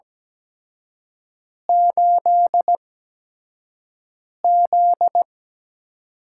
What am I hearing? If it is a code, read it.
Morse code, heard as a keyed tone, about 17 wpm, spaced as for 5 wpm, 700 Hz. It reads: 8Z
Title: USA Call Signs - 4 Characters - 8x Character Spacing - 17wpm